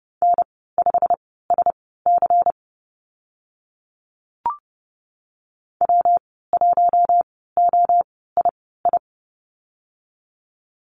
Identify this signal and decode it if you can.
Morse code, heard as a keyed tone, about 30 wpm, spaced as for 17 wpm, 700 Hz. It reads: N5SC W1OII
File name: Lesson 19g - Letter w - Call Signs 30wpm